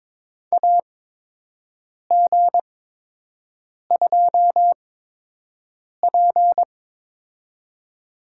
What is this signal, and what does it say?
Morse code, heard as a keyed tone, about 22 wpm, spaced as for 6 wpm, 700 Hz. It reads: AG2P
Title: USA Call Signs - 8x Character Spacing - 22wpm